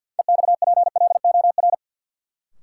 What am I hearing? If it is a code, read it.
Morse code, heard as a keyed tone, about 53 wpm, 700 Hz. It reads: EXPLOR
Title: Sentences from Top 1100 Words - New Word Drill Alt - 50wpm